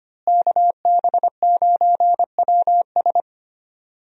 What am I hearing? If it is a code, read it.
Morse code, heard as a keyed tone, about 25 wpm, 700 Hz. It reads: KB9WS
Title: USA Call Signs - 5 Characters - Repeated 2x - 25wpm